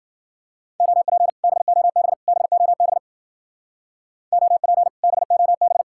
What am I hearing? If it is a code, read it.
Morse code, heard as a keyed tone, about 52 wpm, 690 Hz. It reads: OP BOB BOB OP BOB